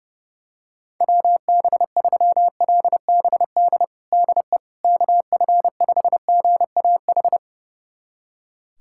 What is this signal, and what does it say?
Morse code, heard as a keyed tone, about 30 wpm, 700 Hz. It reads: WB3LBD DE KF5GAH
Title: USA Callsign DE Callsign 30wpm